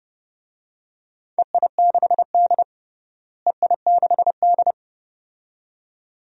Morse code, heard as a keyed tone, about 30 wpm, 700 Hz. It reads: EI6D EI6D
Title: DXCC-100 Call Signs - 4 Characters - Repeated 2x - 30wpm